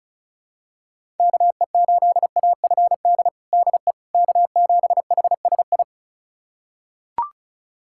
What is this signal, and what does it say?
Morse code, heard as a keyed tone, about 34 wpm, 700 Hz. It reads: KE8AFD DE K7HSI
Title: USA Callsign DE Callsign 35wpm